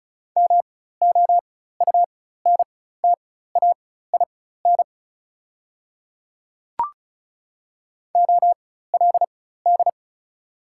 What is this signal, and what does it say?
Morse code, heard as a keyed tone, about 35 wpm, spaced as for 16 wpm, 700 Hz. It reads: MOUNTAIN OLD
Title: 4x Character Spacing - Top 300 Words - 35wpm